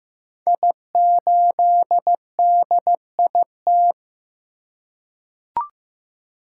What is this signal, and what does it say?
Morse code, heard as a keyed tone, about 15 wpm, 700 Hz. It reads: I8DIT
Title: Code Groups - 5 Letter:Numbers - 15wpm